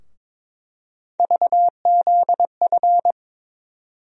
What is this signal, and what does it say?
Morse code, heard as a keyed tone, about 22 wpm, 700 Hz. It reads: VZF
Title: USA Call Sign Postfixes - 3 Letters 22wpm